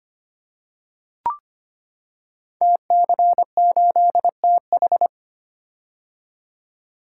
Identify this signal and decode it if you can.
Morse code, heard as a keyed tone, about 25 wpm, 700 Hz. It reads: TC8TH
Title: Code Groups - 5 Letter:Numbers - 25wpm